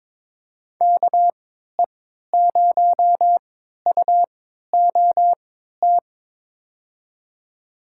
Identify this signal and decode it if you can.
Morse code, heard as a keyed tone, about 22 wpm, spaced as for 13 wpm, 700 Hz. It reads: KE0UOT
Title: USA Call Signs - 6 Characters - 3x Character Spacing - 22wpm